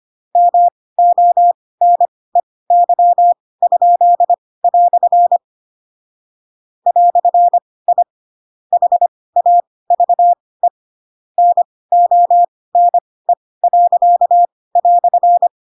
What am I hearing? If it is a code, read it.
Morse code, heard as a keyed tone, about 25 wpm, spaced as for 18 wpm, 700 Hz. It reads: MONEY?" "I HAVE NONE."